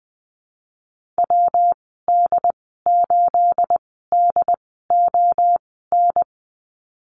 Morse code, heard as a keyed tone, about 20 wpm, spaced as for 15 wpm, 700 Hz. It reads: WD8DON
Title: 238 Lesson 38g - Letter q - Call Signs 20wpm